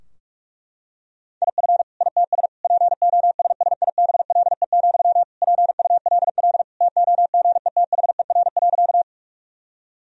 Morse code, heard as a keyed tone, about 41 wpm, 700 Hz. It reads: IF ITS POSSIBLE, PULL TOGETHER.
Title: Sentences from Top 1000 Words - Review Part 2 - 45wpm